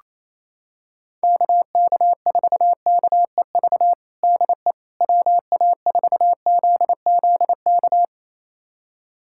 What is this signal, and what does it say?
Morse code, heard as a keyed tone, about 27 wpm, 700 Hz. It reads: KK4KEV DE WA4ZZK
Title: USA Callsign DE Callsign - 6 Characters - 28wpm